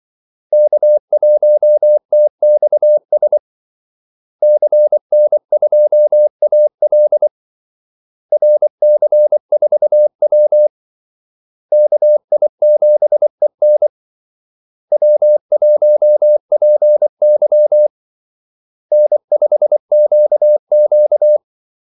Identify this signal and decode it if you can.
Morse code, heard as a keyed tone, about 24 wpm, 600 Hz. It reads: K1TXS CN2AL RC4W KI7EN W1PY N5QQ
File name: Callsigns-24wpm